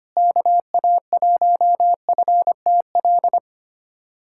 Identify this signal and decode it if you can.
Morse code, heard as a keyed tone, about 25 wpm, 700 Hz. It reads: KA1FTL
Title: USA Callsign DE Callsign - 6 Characters - 25wpm